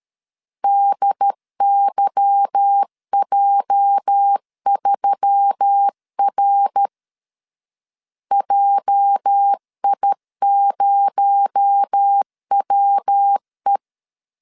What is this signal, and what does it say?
Morse code, heard as a keyed tone, about 13 wpm, 800 Hz. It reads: DYJ3R JI0WE
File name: LEZIONI / 8 YJ45 / 8 YJ 12wpm 2235